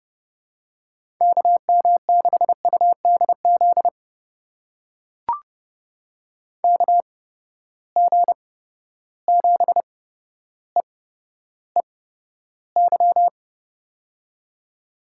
Morse code, heard as a keyed tone, about 30 wpm, spaced as for 13 wpm, 700 Hz. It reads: KM6UDZ KG7EEY
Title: USA Call Signs - 6 Characters - 8x Character Spacing - 30wpm